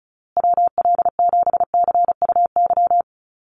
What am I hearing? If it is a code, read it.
Morse code, heard as a keyed tone, about 34 wpm, 700 Hz. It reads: WL7CUY
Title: USA Call Signs Repeated 2x 35wpm